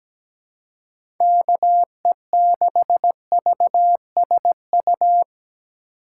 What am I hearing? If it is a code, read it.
Morse code, heard as a keyed tone, about 17 wpm, 700 Hz. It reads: KE6VSU
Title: USA Call Signs - 6 Characters - 4x Character Spacing - 17wpm